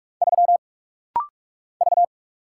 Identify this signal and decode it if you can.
Morse code, heard as a keyed tone, about 45 wpm, 700 Hz. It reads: 3 V
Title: Single Letter-Number - Rapid-Fire 45wpm